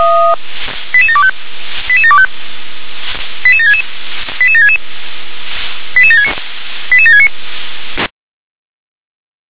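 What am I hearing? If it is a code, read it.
Five-tone selective call (ZVEI), heard as quick runs of five tones, tones of about 80 ms. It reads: 88725 88725 88766 88766 88769 88769
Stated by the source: ZVEI five-tone selective call received on an SDR.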